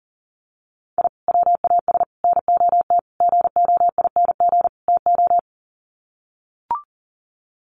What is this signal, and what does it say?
Morse code, heard as a keyed tone, about 40 wpm, 700 Hz. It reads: I WAS NOT GOING TO
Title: Sets of 5 Words 40wpm